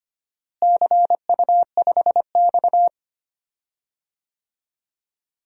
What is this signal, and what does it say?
Morse code, heard as a keyed tone, about 25 wpm, 700 Hz. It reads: CU5X
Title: DXCC-100 Call Signs - 4 Characters - Repeated 2x - 25wpm